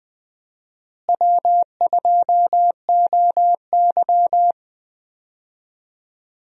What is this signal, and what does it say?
Morse code, heard as a keyed tone, about 20 wpm, 700 Hz. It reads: W2OY
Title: USA Call Signs - 4 Characters - Repeated 3x - 20wpm